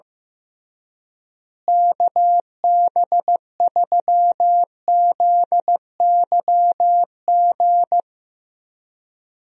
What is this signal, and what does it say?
Morse code, heard as a keyed tone, about 15 wpm, 700 Hz. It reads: KB3ZYG
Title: USA Call Signs - 6 Characters - Repeated 3x - 15wpm